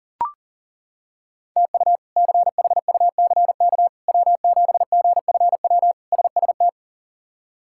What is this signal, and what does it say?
Morse code, heard as a keyed tone, about 40 wpm, 700 Hz. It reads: TU CHUCK W7GFW SST
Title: K1USN SST QSO Practice - 40wpm